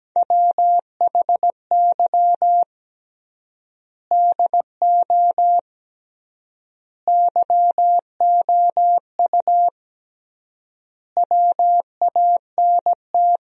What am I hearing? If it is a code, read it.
Morse code, heard as a keyed tone, about 17 wpm, 700 Hz. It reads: WHY DO YOU WANT
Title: 3x Word-Spacing - Sentences from Top 200 Words - 17wpm